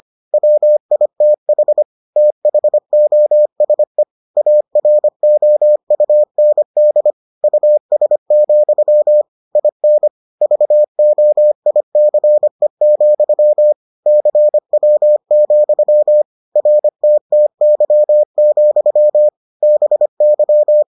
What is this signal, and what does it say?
Morse code, heard as a keyed tone, about 25 wpm, 600 Hz. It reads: WITH THOSE AROUND US, IN VOICE, CW, RTTY, BY